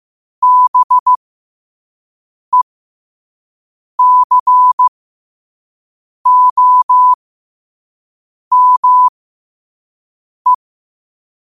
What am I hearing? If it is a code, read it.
Morse code, heard as a keyed tone, about 15 wpm, spaced as for 5 wpm, 1000 Hz. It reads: BECOME
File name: qotdfiles / 5wpm